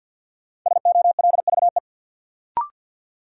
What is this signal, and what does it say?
Morse code, heard as a keyed tone, about 52 wpm, 700 Hz. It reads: SOLVE
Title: Sentences from Top 1000 Words - New Word Drill - 50wpm